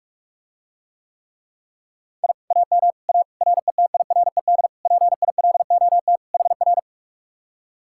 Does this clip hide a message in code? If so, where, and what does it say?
Yes: Morse code from 2.2 to 6.8 s, heard as a keyed tone, about 48 wpm, 700 Hz. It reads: I AM A RETIRED PILOT HR